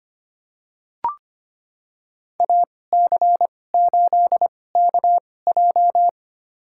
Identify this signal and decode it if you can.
Morse code, heard as a keyed tone, about 25 wpm, spaced as for 18 wpm, 700 Hz. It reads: AC8KJ
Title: USA Call Signs - 2x Character Spacing - 25wpm